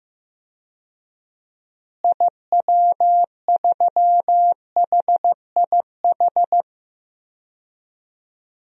Morse code, heard as a keyed tone, about 15 wpm, 700 Hz. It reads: IW3HIH
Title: DXCC-100 Call Signs - 6 Characters - Repeated 2x - 15wpm